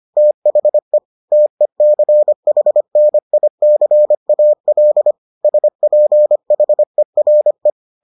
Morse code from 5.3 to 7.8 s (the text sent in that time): SPHERE